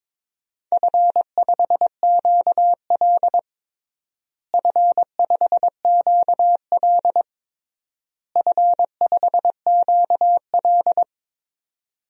Morse code, heard as a keyed tone, about 21 wpm, 700 Hz. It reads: F5QL F5QL F5QL
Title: DXCC-100 Call Signs - 4 Characters - Repeated 3x - 22wpm